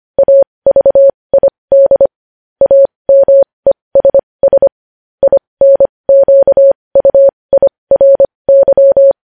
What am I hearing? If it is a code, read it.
Morse code, heard as a keyed tone, about 25 wpm, spaced as for 20 wpm, 550 Hz. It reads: AVID AMESS INQUIRY
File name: podcast / News headlines-1x-25x20